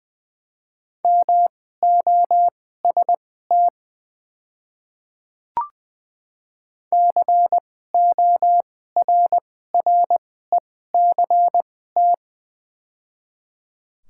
Morse code, heard as a keyed tone, about 21 wpm, spaced as for 15 wpm, 700 Hz. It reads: MOST CORRECT
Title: 2x Character Spacing - Top 500 Words - 20wpm